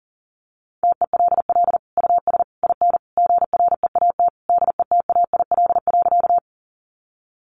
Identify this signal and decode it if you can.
Morse code, heard as a keyed tone, about 40 wpm, 700 Hz. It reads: TELL US IN GREAT DETAIL.
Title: Sentences from Top 1500 Words 40wpm